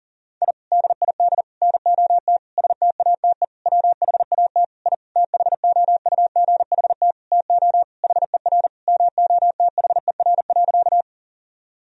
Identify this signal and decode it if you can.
Morse code, heard as a keyed tone, about 40 wpm, 700 Hz. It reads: I DID NOT STATE WHAT I THOUGHT TO HER MOTHER.
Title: Sentences from Top 1000 Words - Review Part 2 - 40wpm